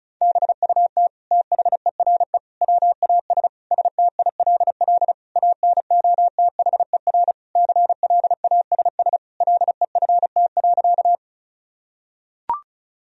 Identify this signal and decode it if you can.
Morse code, heard as a keyed tone, about 34 wpm, 700 Hz. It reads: BUT THERE WAS STILL ANOTHER CLASS LEFT.
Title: Sentences from Top 1000 Words - Review Part 2 - 35wpm